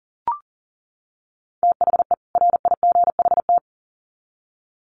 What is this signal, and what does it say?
Morse code, heard as a keyed tone, about 40 wpm, 700 Hz. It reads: THE RIGHT
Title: Sets of 2 Words 40wpm